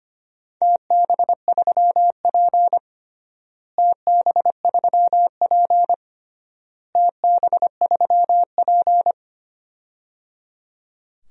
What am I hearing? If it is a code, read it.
Morse code, heard as a keyed tone, about 25 wpm, 700 Hz. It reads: TB3P TB3P TB3P
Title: DXCC-100 Call Signs - 4 Characters - Repeated 3x - 25wpm